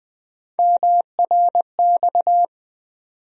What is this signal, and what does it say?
Morse code, heard as a keyed tone, about 21 wpm, 700 Hz. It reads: MRX
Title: USA Call Sign Postfixes - 3 Letters 20wpm